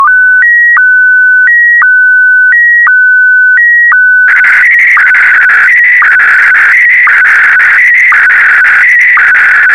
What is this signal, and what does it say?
SSTV (slow-scan television), Scottie DX.